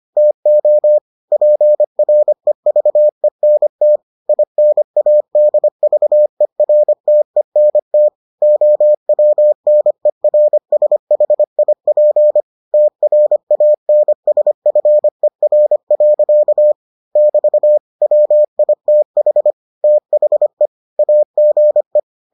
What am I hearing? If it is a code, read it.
Morse code, heard as a keyed tone, about 25 wpm, 600 Hz. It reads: TO PREVENT INADVERTENT OWNERSHIP TRANSFER. <BT> WITH THE AGE